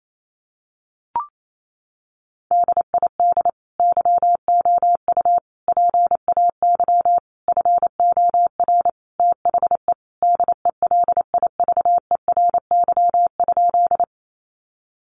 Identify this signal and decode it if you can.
Morse code, heard as a keyed tone, about 28 wpm, 700 Hz. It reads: DID YOU PAY FOR THE DELIVERY?
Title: Sentences from Top 1500 Words 28wpm